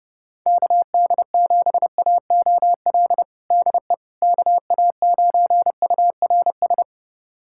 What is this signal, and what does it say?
Morse code, heard as a keyed tone, about 30 wpm, 700 Hz. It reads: KD7AOL DE KA9URS
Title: USA Callsign DE Callsign 30wpm